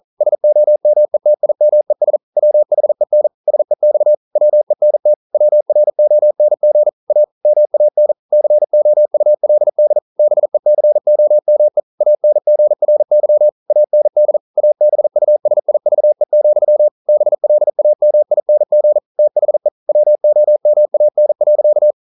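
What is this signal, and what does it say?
Morse code, heard as a keyed tone, about 40 wpm, 600 Hz. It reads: SOMETIMES WHEN SEX WENT WRONG A MAN COULD BECOME ANGRY AND ABUSIVE, BLAMING THE WOMAN.
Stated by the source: Morse code practice at 41 wpm, from CW med Gnister.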